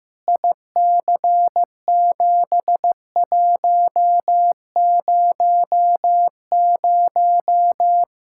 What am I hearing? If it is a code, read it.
Morse code, heard as a keyed tone, about 15 wpm, 700 Hz. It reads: IC7100
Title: K1USN SST QSO Practice 15wpm